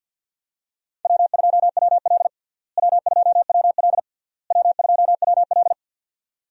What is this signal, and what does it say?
Morse code, heard as a keyed tone, about 45 wpm, 700 Hz. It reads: W2WL W2WL W2WL
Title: USA Call Signs - 4 Characters - Repeated 3x - 50wpm